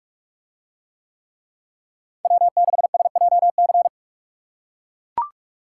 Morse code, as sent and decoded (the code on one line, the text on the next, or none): .-- -.... ... .--- -.-.
W6SJC